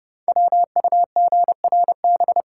Morse code, heard as a keyed tone, about 30 wpm, 700 Hz. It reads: WUGRB
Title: Code Groups - 5 Letters - 30wpm